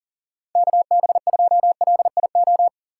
Morse code, heard as a keyed tone, about 38 wpm, 700 Hz. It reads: KB2LIO
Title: USA Call Signs - 3x Character Spacing - 40wpm